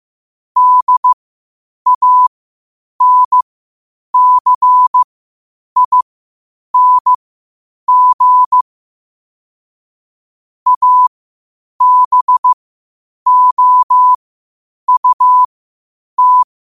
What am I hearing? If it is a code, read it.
Morse code, heard as a keyed tone, about 15 wpm, spaced as for 8 wpm, 1000 Hz. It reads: DANCING ABOUT
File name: qotdfiles / 8wpm